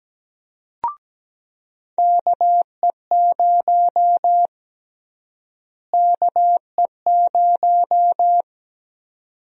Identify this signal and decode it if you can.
Morse code, heard as a keyed tone, about 17 wpm, 700 Hz. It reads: KE0 KE0
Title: USA Call Sign Prefixes - Repeated 2x - 17wpm